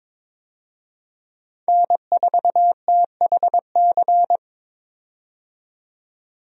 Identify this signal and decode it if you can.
Morse code, heard as a keyed tone, about 22 wpm, 700 Hz. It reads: N4THC
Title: USA Call Signs - 5 Characters - 22wpm